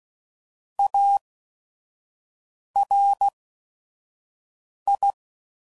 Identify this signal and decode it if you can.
Morse code, heard as a keyed tone, about 16 wpm, spaced as for 5 wpm, 800 Hz. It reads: ARI
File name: CW Code Groups @5WPM / Technical Subjects / XC